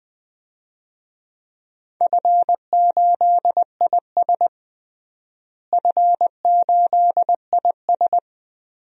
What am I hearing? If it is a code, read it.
Morse code, heard as a keyed tone, about 20 wpm, 700 Hz. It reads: F8IS F8IS
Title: DXCC-100 Call Signs - 4 Characters - Repeated 3x - 20wpm